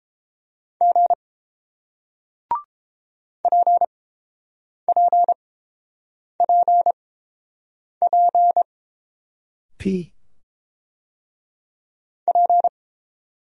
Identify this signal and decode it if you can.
Morse code, heard as a keyed tone, about 29 wpm, spaced as for 7 wpm, 700 Hz. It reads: G PPPP P